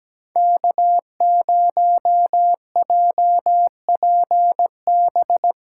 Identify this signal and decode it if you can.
Morse code, heard as a keyed tone, about 17 wpm, 700 Hz. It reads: K0JPB
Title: USA Callsign DE Callsign - 5 Characters - 17wpm